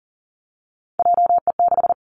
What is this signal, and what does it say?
Morse code, heard as a keyed tone, about 42 wpm, 700 Hz. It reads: JE6
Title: Three Challenging Letter-Number Combinations 40wpm